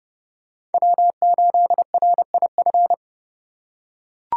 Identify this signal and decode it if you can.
Morse code, heard as a keyed tone, about 30 wpm, 700 Hz. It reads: W8RIF
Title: USA Call Signs - 5 Characters - 2x Character Spacing - 30wpm